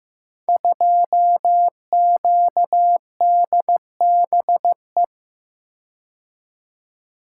Morse code, heard as a keyed tone, about 15 wpm, 700 Hz. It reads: 2QDBE
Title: Code Groups - 5 Letter:Numbers - 15wpm